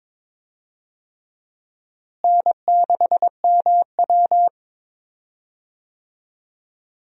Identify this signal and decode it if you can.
Morse code, heard as a keyed tone, about 22 wpm, 700 Hz. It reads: N6MW
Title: USA Call Signs - 4 Characters - Repeated 3x - 22wpm